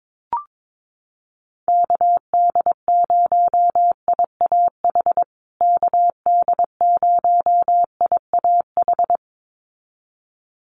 Morse code, heard as a keyed tone, about 22 wpm, 700 Hz. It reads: KD0IAH KD0IAH
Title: USA Call Signs Repeated 2x 22wpm